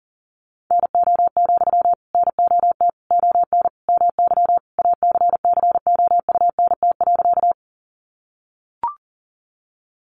Morse code, heard as a keyed tone, about 40 wpm, 700 Hz. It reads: NO, NOT ON MY ACCOUNT.
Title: Sentences from Top 900 Words 40wpm